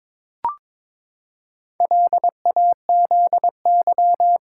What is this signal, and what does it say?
Morse code, heard as a keyed tone, about 22 wpm, 700 Hz. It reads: LAZY
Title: Pangram Challenge v2 - 22wpm